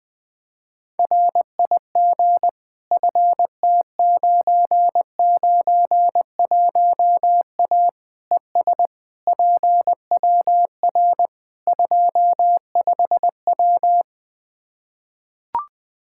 Morse code, heard as a keyed tone, about 20 wpm, 700 Hz. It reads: RIG FT991A ES PWR 25W
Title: Practice QSO - Exchange 2 - Info - 20wpm